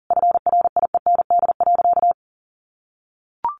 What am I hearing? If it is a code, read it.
Morse code, heard as a keyed tone, about 40 wpm, 700 Hz. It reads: FRIEND.
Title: Sentences from Top 1100 Words 40wpm